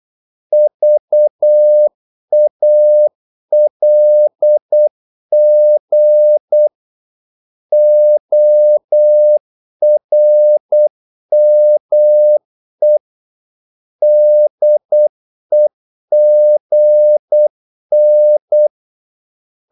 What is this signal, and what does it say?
Morse code, heard as a keyed tone, about 8 wpm, 600 Hz. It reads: VALG ORME DEGN
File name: Korte ord DK 8wpm